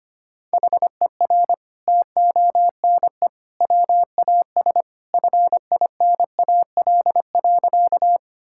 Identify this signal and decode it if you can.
Morse code, heard as a keyed tone, about 25 wpm, 700 Hz. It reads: HER TONE WAS FINAL.
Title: Sentences from Top 1600 Words 25wpm